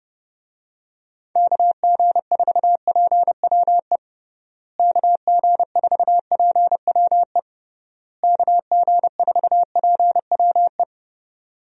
Morse code, heard as a keyed tone, about 32 wpm, 700 Hz. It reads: KG4PWE KG4PWE KG4PWE
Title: USA Call Signs - 6 Characters - Repeated 3x - 30wpm